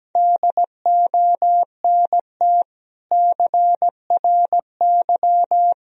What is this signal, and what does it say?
Morse code, heard as a keyed tone, about 17 wpm, 700 Hz. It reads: DONT CRY
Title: Sentences from Top 1000 Words - Review Part 2 - 17wpm